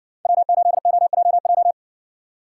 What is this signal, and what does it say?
Morse code, heard as a keyed tone, about 60 wpm, 700 Hz. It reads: W8OJJ